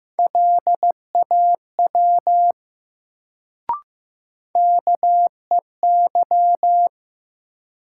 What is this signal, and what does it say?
Morse code, heard as a keyed tone, about 15 wpm, 700 Hz. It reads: LAW KEY
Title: Sentences from Top 600 Words - New Word Drill - 15wpm